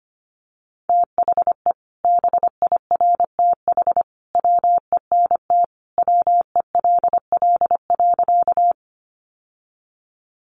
Morse code, heard as a keyed tone, about 25 wpm, 700 Hz. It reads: THE BIRTH WENT WELL.
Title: Sentences from Top 800 Words 25wpm